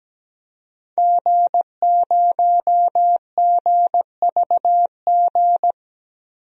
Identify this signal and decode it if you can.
Morse code, heard as a keyed tone, about 17 wpm, 700 Hz. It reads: G0GVG
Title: DXCC-100 Call Signs - 5 Characters - 17wpm